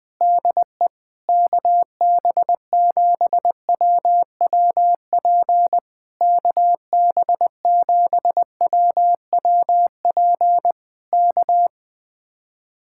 Morse code, heard as a keyed tone, about 20 wpm, 700 Hz. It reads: DE KB7WWP KB7WWP K
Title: Speed-Racing Common QSO Phrases - 17wpm